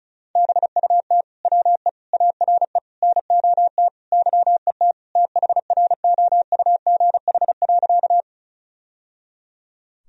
Morse code, heard as a keyed tone, about 34 wpm, 700 Hz. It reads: BUT WE ARE NOT YET THROUGH.
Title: Sentences from Top 1000 Words - Review Part 1 - 35wpm